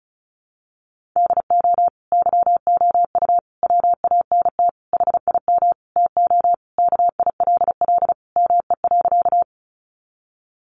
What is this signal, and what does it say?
Morse code, heard as a keyed tone, about 34 wpm, 700 Hz. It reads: DO YOU WANT HIM TO KILL ME.
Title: Sentences from Top 1400 Words 35wpm